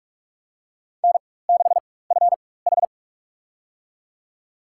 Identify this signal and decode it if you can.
Morse code, heard as a keyed tone, about 41 wpm, spaced as for 20 wpm, 700 Hz. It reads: N6FH
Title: USA Call Signs - 4 Characters - 4x Character Spacing - 45wpm